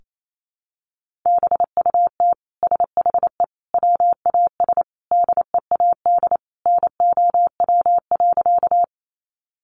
Morse code, heard as a keyed tone, about 28 wpm, 700 Hz. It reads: BUT SHE WAS DEAD NOW.
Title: Sentences from Top 1500 Words 28wpm